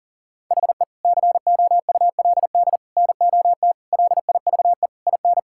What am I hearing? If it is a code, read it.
Morse code, heard as a keyed tone, about 40 wpm, 700 Hz. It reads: HE COULD NOT LIVE IN
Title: Sentences from Top 1600 Words 40wpm